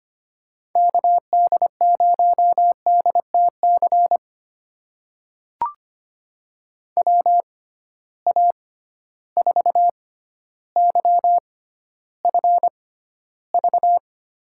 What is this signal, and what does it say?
Morse code, heard as a keyed tone, about 25 wpm, spaced as for 13 wpm, 700 Hz. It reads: KD0DTC WA4YFV
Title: USA Call Signs - 6 Characters - 6x Character Spacing - 25wpm